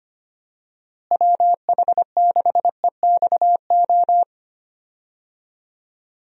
Morse code, heard as a keyed tone, about 25 wpm, 700 Hz. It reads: WH6EXO
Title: USA Call Signs - 6 Characters - Repeated 3x - 25wpm